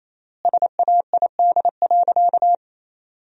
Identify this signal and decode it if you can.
Morse code, heard as a keyed tone, about 28 wpm, 700 Hz. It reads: SAID.
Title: 3x Word-Spacing - Sentences from Top 200 Words - 28wpm